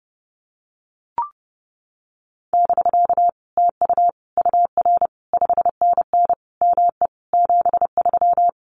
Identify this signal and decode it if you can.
Morse code, heard as a keyed tone, about 30 wpm, 700 Hz. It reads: <BK> TU UR 5NN ME 73
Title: Common QSO Phrases Encore 30wpm